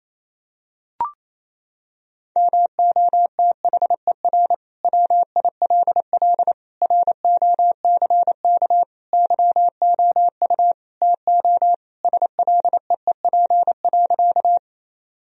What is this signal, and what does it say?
Morse code, heard as a keyed tone, about 28 wpm, 700 Hz. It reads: MOTHER WILL ROCK YOU TO SLEEP.
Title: Sentences from Top 1000 Words - Review Part 2 - 28wpm